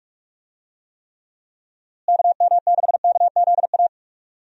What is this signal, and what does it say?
Morse code, heard as a keyed tone, about 45 wpm, 700 Hz. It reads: KM6KZA
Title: USA Call Signs - 6 Characters - Repeated 2x - 45wpm